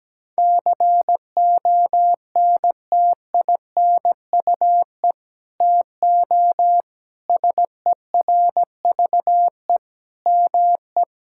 Morse code, heard as a keyed tone, about 17 wpm, 700 Hz. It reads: CONTINUE TO SERVE ME
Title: Sentences from Top 1000 Words - Review Part 6 - 17wpm